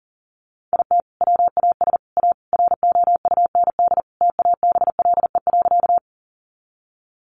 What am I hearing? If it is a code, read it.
Morse code, heard as a keyed tone, about 40 wpm, 700 Hz. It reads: IT WAS A ROUND TABLE.
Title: Sentences from Top 400 Words 40wpm